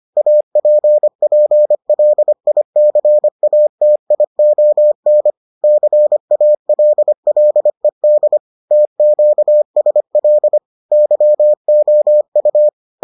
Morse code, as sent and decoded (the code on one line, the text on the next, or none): .- .--. .--. .-.. .. -.-. .- - .. --- -. / -.-. .- .-.. .-.. -.. / - --.- ... .-.. / -.-- --- ..-
APPLICATION CALLED TQSL YOU